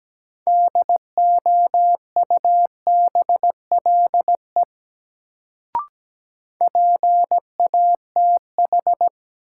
Morse code, heard as a keyed tone, about 17 wpm, 700 Hz. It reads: DOUBLE PATH
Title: Sentences from Top 1400 Words - New Word Drill Alt - 17wpm